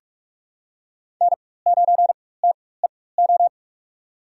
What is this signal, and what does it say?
Morse code, heard as a keyed tone, about 46 wpm, spaced as for 21 wpm, 700 Hz. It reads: N9TEO
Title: USA Call Signs - 4x Character Spacing - 45wpm